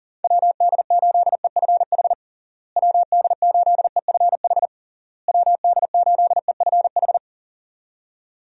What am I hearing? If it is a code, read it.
Morse code, heard as a keyed tone, about 40 wpm, 700 Hz. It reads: WD8EFH WD8EFH WD8EFH
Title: USA Call Signs - 6 Characters - Repeated 3x - 40wpm